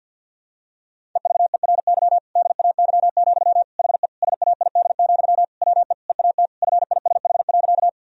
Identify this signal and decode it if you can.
Morse code, heard as a keyed tone, about 48 wpm, 700 Hz. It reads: EVERY DAY, HE SAID, WE EAT FISH.
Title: Sentences from Top 1000 Words - Review Part 1 - 50wpm